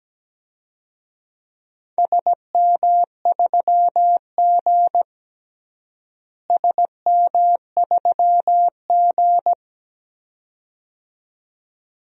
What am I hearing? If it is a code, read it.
Morse code, heard as a keyed tone, about 17 wpm, 700 Hz. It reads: SM3G SM3G
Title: DXCC-100 Call Signs - 4 Characters - Repeated 2x - 17wpm